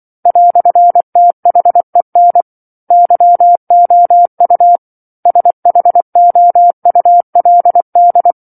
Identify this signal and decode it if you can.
Morse code, heard as a keyed tone, about 24 wpm, 700 Hz. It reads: "THEN YOU SHOULD
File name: 07-Chapter-07 24 WPM